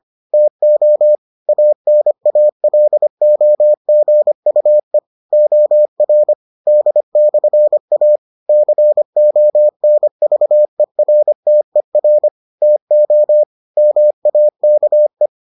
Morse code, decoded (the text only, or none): TO ANALOGUE OR D/A CONVERTER TO MAKE